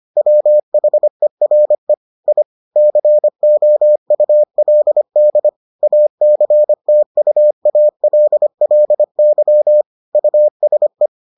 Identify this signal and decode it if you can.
Morse code, heard as a keyed tone, about 25 wpm, 600 Hz. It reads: WHERE I COULD ACTUALLY USE